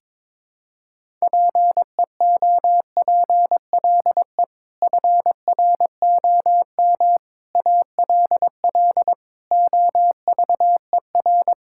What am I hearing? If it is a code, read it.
Morse code, heard as a keyed tone, about 22 wpm, 700 Hz. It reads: PEOPLE FROM ALL OVER
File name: practice / 22wpm / Sets of 6 Words - 22wpm